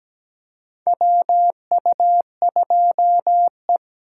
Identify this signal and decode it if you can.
Morse code, heard as a keyed tone, about 17 wpm, 700 Hz. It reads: WU2E
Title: USA Call Signs - 4x Character Spacing - 17wpm